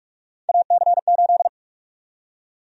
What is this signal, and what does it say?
Morse code, heard as a keyed tone, about 42 wpm, 700 Hz. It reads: AC8
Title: USA Call Sign Prefixes - 45wpm